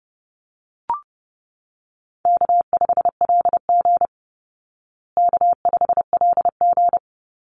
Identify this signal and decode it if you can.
Morse code, heard as a keyed tone, about 30 wpm, 700 Hz. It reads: K5LG K5LG
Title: Call Signs Repeated 3x - 30wpm